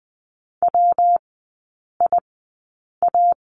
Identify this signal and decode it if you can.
Morse code, heard as a keyed tone, about 20 wpm, spaced as for 8 wpm, 700 Hz. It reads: WIA
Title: 178 Lesson 29e - Slash - Three Characters 20wpm